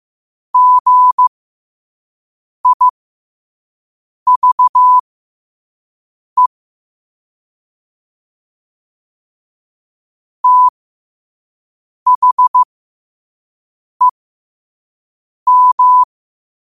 Morse code, heard as a keyed tone, about 15 wpm, spaced as for 5 wpm, 1000 Hz. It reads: GIVE THEM
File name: qotdfiles / 5wpm